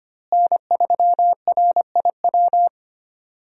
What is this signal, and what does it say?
Morse code, heard as a keyed tone, about 25 wpm, 700 Hz. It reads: N3RIW
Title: USA Call Signs - 5 Characters - Repeated 3x - 25wpm